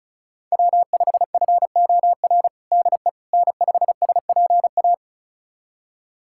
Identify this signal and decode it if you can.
Morse code, heard as a keyed tone, about 36 wpm, 690 Hz. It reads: W5FOR DE N5SPA
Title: USA Callsign DE Callsign - 5 Characters - 35wpm